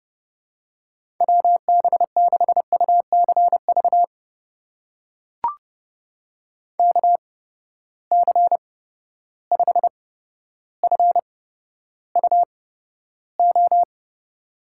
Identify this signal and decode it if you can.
Morse code, heard as a keyed tone, about 31 wpm, spaced as for 13 wpm, 700 Hz. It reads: WB6UCV KC5FUO
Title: USA Call Signs - 8x Character Spacing - 30wpm